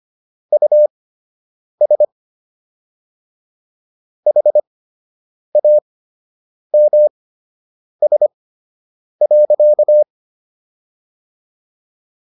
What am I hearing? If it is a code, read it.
Morse code, heard as a keyed tone, about 25 wpm, spaced as for 8 wpm, 600 Hz. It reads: US HAMS.